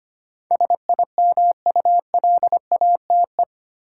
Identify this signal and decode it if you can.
Morse code, heard as a keyed tone, about 25 wpm, 700 Hz. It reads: SIMULATE
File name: Eight Letter Words - 25wpm